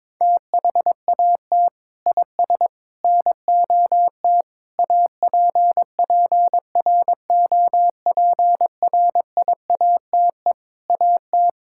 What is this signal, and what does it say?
Morse code, heard as a keyed tone, about 22 wpm, 700 Hz. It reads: THAT IS NOT APPROPRIATE AT
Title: Sentences from Top 1000 Words - Review Part 5 - 22wpm